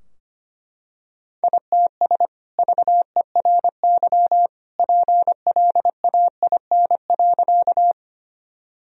Morse code, heard as a keyed tone, about 25 wpm, 700 Hz. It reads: ITS VERY PLAIN.
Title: Sentences from Top 1000 Words - Review Part 2 - 25wpm